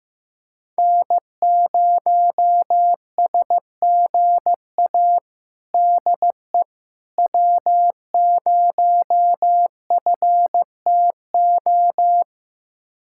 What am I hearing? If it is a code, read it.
Morse code, heard as a keyed tone, about 15 wpm, 700 Hz. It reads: N0SGA DE W0FTO
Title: USA Callsign DE Callsign - 5 Characters - 15wpm